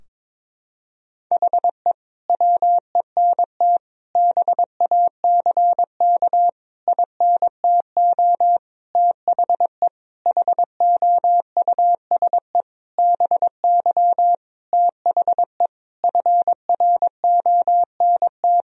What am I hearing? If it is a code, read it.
Morse code, heard as a keyed tone, about 22 wpm, 700 Hz. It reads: HE WENT BACK INTO THE HOUSE BY THE FRONT
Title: Sentences from Top 1000 Words - Review Part 2 - 22wpm